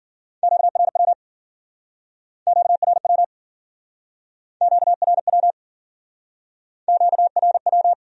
Morse code, heard as a keyed tone, about 56 wpm, 700 Hz. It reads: QRW QRW QRW QRW